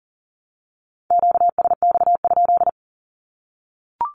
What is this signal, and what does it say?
Morse code, heard as a keyed tone, about 40 wpm, 700 Hz. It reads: QSX?
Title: Q-Codes 40wpm